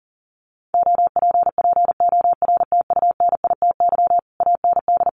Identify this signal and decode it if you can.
Morse code, heard as a keyed tone, about 40 wpm, 700 Hz. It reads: OPPORTUNITY AND
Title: Sentences from Top 1200 Words 40wpm